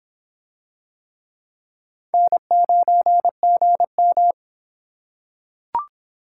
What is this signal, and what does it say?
Morse code, heard as a keyed tone, about 23 wpm, 700 Hz. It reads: N9GM